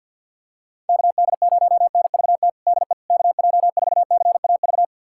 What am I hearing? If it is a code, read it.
Morse code, heard as a keyed tone, about 51 wpm, 700 Hz. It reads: KD0NVT DE KJ4CAV
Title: USA Callsign DE Callsign 50wpm